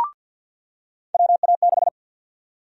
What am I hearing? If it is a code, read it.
Morse code, heard as a keyed tone, about 50 wpm, 700 Hz. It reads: WA6
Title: USA Call Sign Prefixes - 50wpm